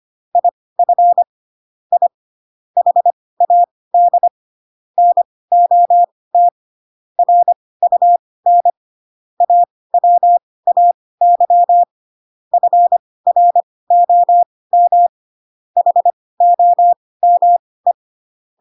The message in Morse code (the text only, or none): IF I HAD NOT RUN AWAY FROM HOME